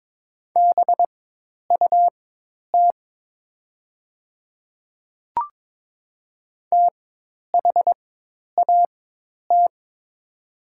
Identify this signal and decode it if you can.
Morse code, heard as a keyed tone, about 22 wpm, spaced as for 10 wpm, 700 Hz. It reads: BUT THAT
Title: 4x Character Spacing - Top 100 Words - 22wpm